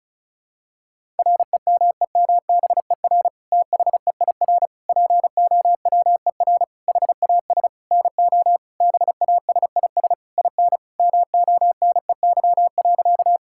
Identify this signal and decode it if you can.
Morse code, heard as a keyed tone, about 34 wpm, 700 Hz. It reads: REMEMBER THEIR POWER HAS NO BASIS IN MONEY.
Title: Sentences from Top 1000 Words - Review Part 5 - 35wpm